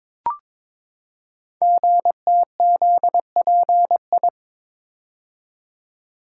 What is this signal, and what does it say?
Morse code, heard as a keyed tone, about 23 wpm, 700 Hz. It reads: GTZPI
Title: Code Groups - 5 Letters - 22wpm